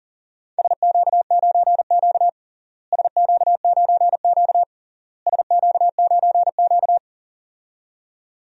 Morse code, heard as a keyed tone, about 38 wpm, 700 Hz. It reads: SQ9Q SQ9Q SQ9Q
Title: DXCC-100 Call Signs - 4 Characters - Repeated 3x - 40wpm